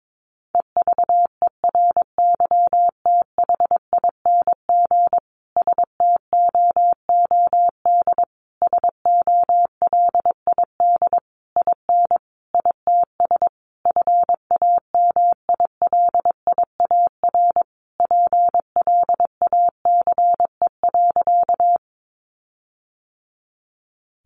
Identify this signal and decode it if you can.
Morse code, heard as a keyed tone, about 22 wpm, 700 Hz. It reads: EVERYTHING STOOD SOLID IN ITS FAMILIAR PLACE.
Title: Sentences from Top 1500 Words 22wpm